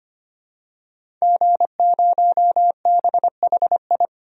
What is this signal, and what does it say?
Morse code, heard as a keyed tone, about 25 wpm, 700 Hz. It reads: G0BHI
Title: DXCC-100 Call Signs - 5 Characters - 25wpm